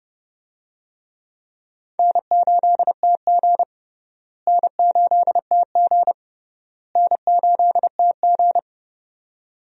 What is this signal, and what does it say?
Morse code, heard as a keyed tone, about 32 wpm, 700 Hz. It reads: N8TG N8TG N8TG
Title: USA Call Signs - 4 Characters - Repeated 3x - 30wpm